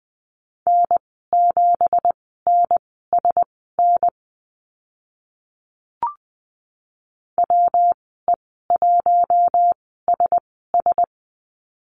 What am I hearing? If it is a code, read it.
Morse code, heard as a keyed tone, about 20 wpm, spaced as for 15 wpm, 700 Hz. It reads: N7NSN WE1SS
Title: 219 Lesson 35g - Number 0 - Call Signs 20wpm